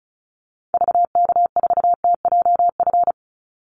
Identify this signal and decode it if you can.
Morse code, heard as a keyed tone, about 34 wpm, 700 Hz. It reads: VK4TJF
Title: Call Signs Repeated 3x - 35wpm